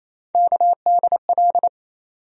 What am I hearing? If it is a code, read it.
Morse code, heard as a keyed tone, about 28 wpm, 700 Hz. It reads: KDL
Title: USA Call Sign Postfixes - 3 Letters 28wpm